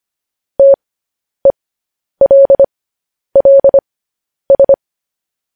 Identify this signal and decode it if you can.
Morse code, heard as a keyed tone, about 25 wpm, spaced as for 10 wpm, 550 Hz. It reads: TELLS
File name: podcast / News headlines-1x-25x10